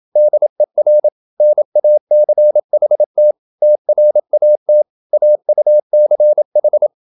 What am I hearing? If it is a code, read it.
Morse code, heard as a keyed tone, about 27 wpm, 600 Hz. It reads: DER NACHT TRAT AUCH